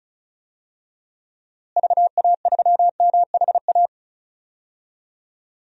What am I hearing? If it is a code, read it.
Morse code, heard as a keyed tone, about 36 wpm, 700 Hz. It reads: VA3MHA
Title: DXCC-100 Call Signs - 6 Characters - Repeated 2x - 35wpm